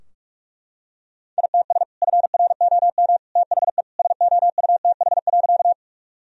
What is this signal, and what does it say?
Morse code, heard as a keyed tone, about 42 wpm, 700 Hz. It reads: ITS FROM THE SOUTH.
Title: Sentences from Top 1000 Words - Review Part 2 - 45wpm